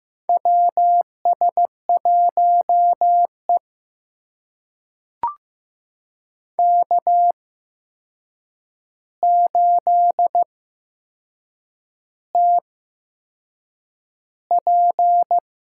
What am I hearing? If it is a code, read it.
Morse code, heard as a keyed tone, about 15 wpm, spaced as for 6 wpm, 700 Hz. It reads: WS1E K8TP
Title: USA Call Signs - 4 Characters - 8x Character Spacing - 15wpm